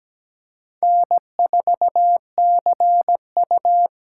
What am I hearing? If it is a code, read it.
Morse code, heard as a keyed tone, about 17 wpm, 700 Hz. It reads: N4CU
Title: USA Call Signs - 4 Characters - 4x Character Spacing - 17wpm